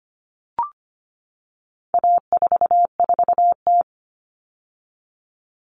Morse code, heard as a keyed tone, about 25 wpm, 700 Hz. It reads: A44T
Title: All Cut Numbers - 4-Digit 25wpm